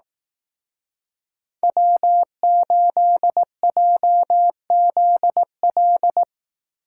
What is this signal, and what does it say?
Morse code, heard as a keyed tone, about 18 wpm, 700 Hz. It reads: W8JZL
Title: Speed Racing - USA 1x3 Call Signs 15wpm